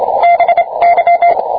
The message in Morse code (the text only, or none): BY